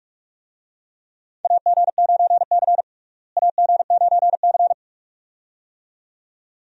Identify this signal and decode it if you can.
Morse code, heard as a keyed tone, about 46 wpm, 700 Hz. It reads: AG9C AG9C
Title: USA Call Signs - 4 Characters - Repeated 2x - 45wpm